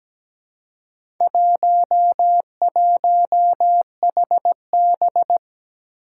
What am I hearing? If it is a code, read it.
Morse code, heard as a keyed tone, about 17 wpm, 700 Hz. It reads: 11HB
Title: Four Challenging Letter-Number Combinations 17wpm